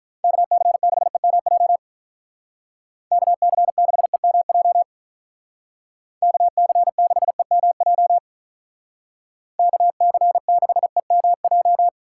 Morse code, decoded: KC6EMJ KC6EMJ KC6EMJ KC6EMJ

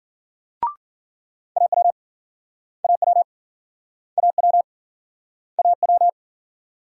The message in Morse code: AW AW AW AW

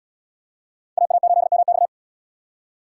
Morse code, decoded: AA8MX

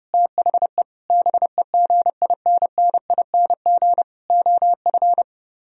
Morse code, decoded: THE BEGINNING OF

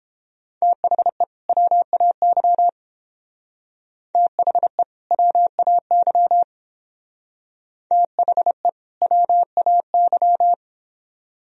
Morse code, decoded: THE WAY THE WAY THE WAY

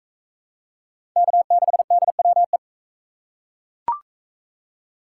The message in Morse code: K6DWE

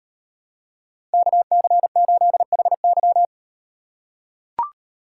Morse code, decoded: KC8HY

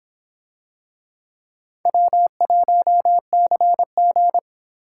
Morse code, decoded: W1CG

